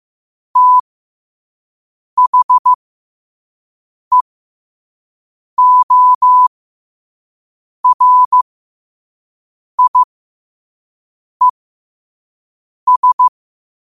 Morse code, heard as a keyed tone, about 15 wpm, spaced as for 5 wpm, 1000 Hz. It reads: THEORIES
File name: qotdfiles / 5wpm